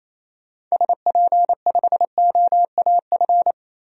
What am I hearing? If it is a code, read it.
Morse code, heard as a keyed tone, about 28 wpm, 700 Hz. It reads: SP5OAF
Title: DXCC-100 Call Signs - 6 Characters - 28wpm